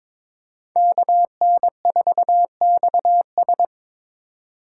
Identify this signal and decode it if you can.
Morse code, heard as a keyed tone, about 23 wpm, 700 Hz. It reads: KN4XS
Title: USA Call Signs - 3x Character Spacing - 22wpm